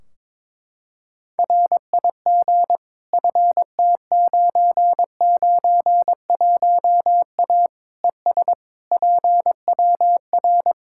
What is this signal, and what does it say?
Morse code, heard as a keyed tone, about 22 wpm, 700 Hz. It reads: RIG FT991A ES PWR
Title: Practice QSO - Exchange 2 - Info - 22wpm